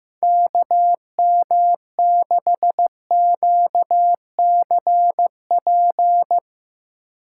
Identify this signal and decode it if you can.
Morse code, heard as a keyed tone, about 16 wpm, 700 Hz. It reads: KM6QCP
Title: Speed Racing - USA 2x3 Call Signs 15wpm